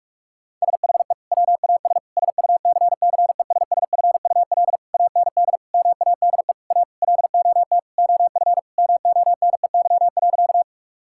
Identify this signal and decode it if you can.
Morse code, heard as a keyed tone, about 41 wpm, 700 Hz. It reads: SHE WAS SUCCESSFUL AND MADE A LOT OF MONEY.
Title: Sentences from Top 1000 Words - Review Part 6 - 45wpm